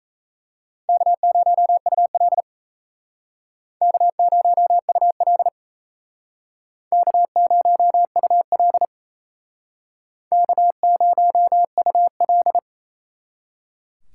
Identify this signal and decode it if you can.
Morse code, heard as a keyed tone, about 35 wpm, 700 Hz. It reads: K0UL K0UL K0UL K0UL
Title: Speed Racing - USA 1x2 Call Signs 28wpm